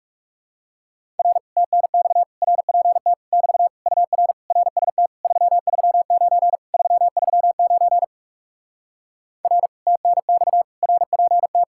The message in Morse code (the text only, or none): R TNX RPT <BT> UR RST 339 339 R TNX RPT